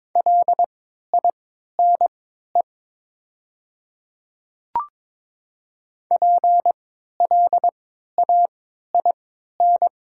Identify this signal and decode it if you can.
Morse code, heard as a keyed tone, about 22 wpm, spaced as for 13 wpm, 700 Hz. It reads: LINE PLAIN
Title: 3x Character Spacing - Top 500 Words - 22wpm